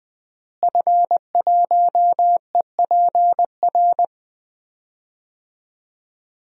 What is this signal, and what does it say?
Morse code, heard as a keyed tone, about 21 wpm, 700 Hz. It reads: F1EPR
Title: DXCC-100 Call Signs - 5 Characters - Repeated 3x - 20wpm